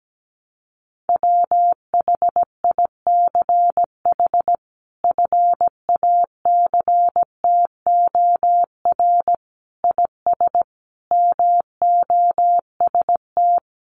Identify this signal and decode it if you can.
Morse code, heard as a keyed tone, about 17 wpm, 700 Hz. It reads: WHICH FACTOR IS MOST
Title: Sentences from Top 1300 Words 17wpm